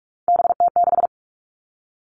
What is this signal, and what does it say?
Morse code, heard as a keyed tone, about 44 wpm, 700 Hz. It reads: BT6
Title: All Cut Numbers - 3-Digit 45wpm